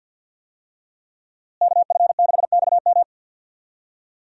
Morse code, heard as a keyed tone, about 52 wpm, 700 Hz. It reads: KF6XM
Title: USA Call Signs - 5 Characters - Repeated 3x - 50wpm